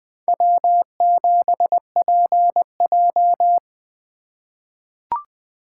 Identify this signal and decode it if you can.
Morse code, heard as a keyed tone, about 20 wpm, 700 Hz. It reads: W7PJ
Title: USA Call Signs - 4 Characters - 3x Character Spacing - 20wpm